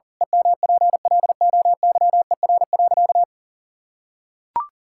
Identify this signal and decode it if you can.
Morse code, heard as a keyed tone, about 40 wpm, 700 Hz. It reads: EMPLOYER.
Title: Pangram Challenge v2 - 40wpm